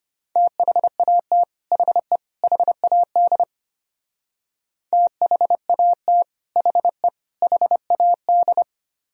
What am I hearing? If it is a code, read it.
Morse code, heard as a keyed tone, about 29 wpm, 700 Hz. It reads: THAT HE HAD THAT HE HAD
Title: Speed-Racing - Sets of 3 Words 25wpm